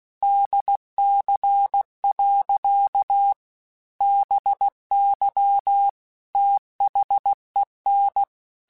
Morse code, heard as a keyed tone, about 16 wpm, 800 Hz. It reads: DC. BY THEN
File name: Technical Subjects / Technical Subjects-mp3s / Sw Pwr Sup @15